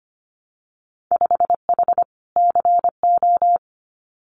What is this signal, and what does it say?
Morse code, heard as a keyed tone, about 25 wpm, 690 Hz. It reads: 5H CO
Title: Winter Field Day Exchanges Encore 25wpm